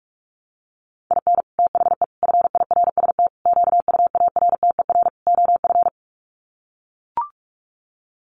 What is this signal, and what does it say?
Morse code, heard as a keyed tone, about 44 wpm, 700 Hz. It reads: IN THE FIRST QUARTER OF
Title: Sets of 5 Words Encore2 45wpm